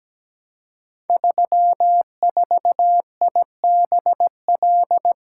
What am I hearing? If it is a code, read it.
Morse code, heard as a keyed tone, about 17 wpm, 700 Hz. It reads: 34IBL
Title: Code Groups - 5 Letter:Numbers - 17wpm